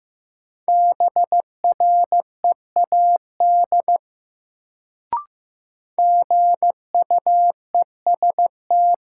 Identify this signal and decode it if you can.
Morse code, heard as a keyed tone, about 16 wpm, 700 Hz. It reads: BREAD GUEST
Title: Sentences from Top 1600 Words - New Word Drill Alt - 15wpm